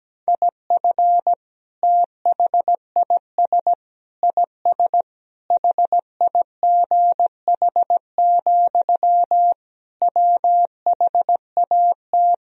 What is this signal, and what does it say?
Morse code, heard as a keyed tone, about 17 wpm, 700 Hz. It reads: IF THIS IS HIGH, WHAT
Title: Sentences from Top 1000 Words - Review Part 1 - 17wpm